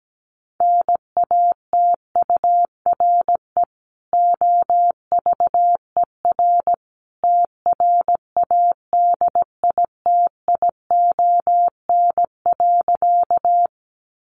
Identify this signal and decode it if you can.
Morse code, heard as a keyed tone, about 17 wpm, 700 Hz. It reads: NATURE OVER TRADITION.
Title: Sentences from Top 1500 Words 17wpm